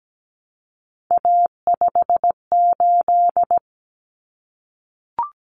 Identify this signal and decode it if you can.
Morse code, heard as a keyed tone, about 17 wpm, 700 Hz. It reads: A58
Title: Common Cut Numbers - 3-Digit 17wpm